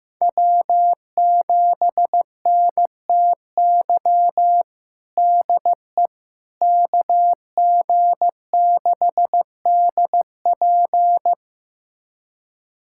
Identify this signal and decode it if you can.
Morse code, heard as a keyed tone, about 15 wpm, 700 Hz. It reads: W7NTY DE KG6DP
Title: USA Callsign DE Callsign - 5 Characters - 15wpm